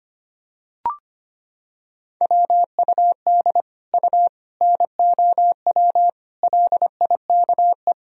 Morse code, heard as a keyed tone, about 25 wpm, 700 Hz. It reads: WUD U NOW LIKE
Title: Sentences from Top 200 Words and Abbreviations 25wpm